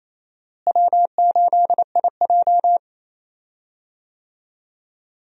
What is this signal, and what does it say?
Morse code, heard as a keyed tone, about 28 wpm, 700 Hz. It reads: W8IJ
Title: USA Call Signs - 4 Characters - Repeated 2x - 28wpm